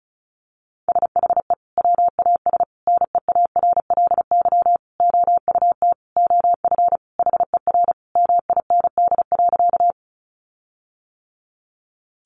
Morse code, heard as a keyed tone, about 34 wpm, 700 Hz. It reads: SHE WAS NEARLY OUT OF HER MIND.
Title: Sentences from Top 1300 Words 35wpm